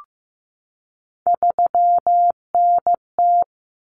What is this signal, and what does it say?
Morse code, heard as a keyed tone, about 15 wpm, 700 Hz. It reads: 3NT
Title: Common Cut Numbers - 3-Digit 15wpm